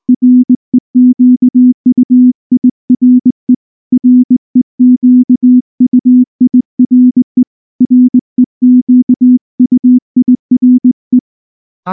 Morse code, heard as a keyed tone, about 19 wpm, 250 Hz. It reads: REQUIRE REQUIRE REQUIRE
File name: morse / morse-training-20WPM-5